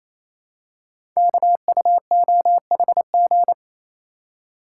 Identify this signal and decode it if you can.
Morse code, heard as a keyed tone, about 27 wpm, 700 Hz. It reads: KUOHG
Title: Code Groups - 5 Letters - 28wpm